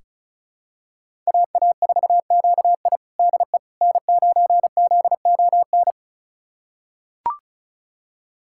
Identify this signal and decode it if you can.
Morse code, heard as a keyed tone, about 36 wpm, 700 Hz. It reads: AA4QI DE N9ZON
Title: USA Callsign DE Callsign - 5 Characters - 35wpm